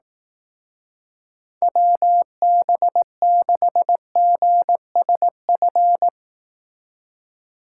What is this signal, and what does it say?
Morse code, heard as a keyed tone, about 18 wpm, 700 Hz. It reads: WB6GSF
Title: Speed Racing - USA Call Signs 15wpm